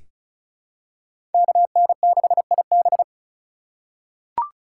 Morse code, heard as a keyed tone, about 34 wpm, 700 Hz. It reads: KN6IB
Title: USA Call Signs - 5 Characters - 3x Character Spacing - 35wpm